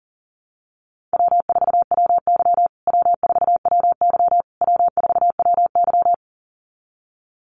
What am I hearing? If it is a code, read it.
Morse code, heard as a keyed tone, about 37 wpm, 700 Hz. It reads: W4WY W4WY W4WY
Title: USA Call Signs Repeated 3x 40wpm